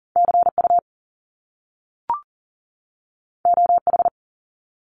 Morse code, heard as a keyed tone, about 38 wpm, 700 Hz. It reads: CU OH
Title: Top Two Letter Combinations in Words 40wpm